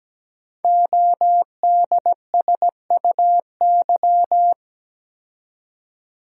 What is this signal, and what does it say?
Morse code, heard as a keyed tone, about 17 wpm, 700 Hz. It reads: ODSUY
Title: Code Groups - 5 Letters - 17wpm